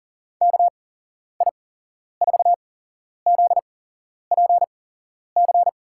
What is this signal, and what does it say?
Morse code, heard as a keyed tone, about 38 wpm, spaced as for 11 wpm, 700 Hz. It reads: KI4ZPC
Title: USA Call Signs - 6 Characters - 8x Character Spacing - 40wpm